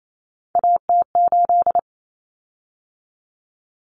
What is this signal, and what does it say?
Morse code, heard as a keyed tone, about 28 wpm, 700 Hz. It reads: AT8
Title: Common Cut Numbers - 3-Digit 28wpm